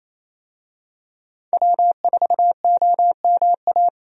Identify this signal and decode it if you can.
Morse code, heard as a keyed tone, about 28 wpm, 700 Hz. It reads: W4OMA
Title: USA Call Signs - 5 Characters - 28wpm